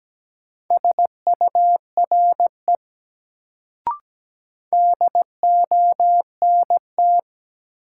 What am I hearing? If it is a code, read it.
Morse code, heard as a keyed tone, about 17 wpm, 700 Hz. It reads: SURE DONT
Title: Sentences from Top 300 Words - New Word Drill - 17wpm